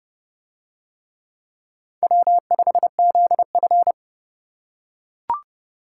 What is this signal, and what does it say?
Morse code, heard as a keyed tone, about 29 wpm, 700 Hz. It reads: W5ZF